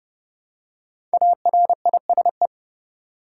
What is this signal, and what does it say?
Morse code, heard as a keyed tone, about 32 wpm, 700 Hz. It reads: ARISE
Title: Five Letter Words - 30wpm